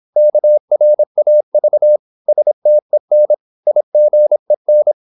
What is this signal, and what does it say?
Morse code, heard as a keyed tone, about 26 wpm, 600 Hz. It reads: KRAV STEN IGEN
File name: Korte ord DK 26wpm